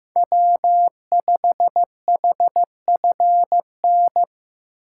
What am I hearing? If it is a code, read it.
Morse code, heard as a keyed tone, about 15 wpm, 700 Hz. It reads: W5HFN
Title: USA Call Signs - 4x Character Spacing - 15wpm